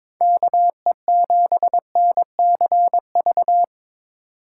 Morse code, heard as a keyed tone, about 26 wpm, 700 Hz. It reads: KE7NCV
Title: Speed Racing - USA 2x3 Call Signs 22wpm